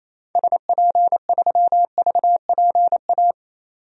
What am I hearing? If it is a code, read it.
Morse code, heard as a keyed tone, about 28 wpm, 700 Hz. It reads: SP3VPA
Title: DXCC-100 Call Signs - 6 Characters - 28wpm